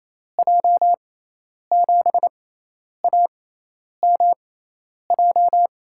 Morse code, heard as a keyed tone, about 28 wpm, spaced as for 10 wpm, 700 Hz. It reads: J7AMJ
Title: USA Call Signs - 6 Characters - 6x Character Spacing - 28wpm